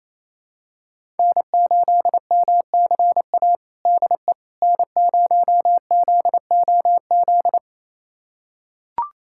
Morse code, heard as a keyed tone, about 28 wpm, 700 Hz. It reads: N8MCA DE N0ZOZ
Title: USA Callsign DE Callsign - 5 Characters - 28wpm